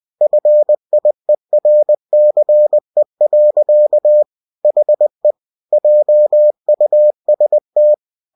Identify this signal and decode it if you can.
Morse code, heard as a keyed tone, about 20 wpm, 600 Hz. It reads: FIERCE. HE JUST
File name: cw-books / winnie-the-pooh / winnie-20wpm-page-0079